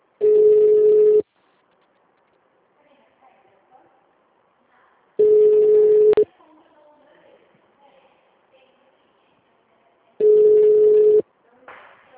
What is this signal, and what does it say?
Ringback tone (Europe, 425 Hz), with speech.
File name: out- -2007- - - .12446